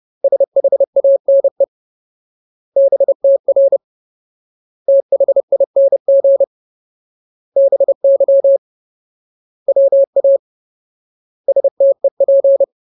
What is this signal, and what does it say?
Morse code, heard as a keyed tone, about 30 wpm, 550 Hz. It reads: SHANE BTR THING BY WA STEP
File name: Ultimate Word List / 30WPM Ultimate